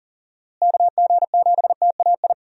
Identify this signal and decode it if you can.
Morse code, heard as a keyed tone, about 40 wpm, 700 Hz. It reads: KG7TAI
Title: USA Call Signs - 6 Characters - Repeated 3x - 40wpm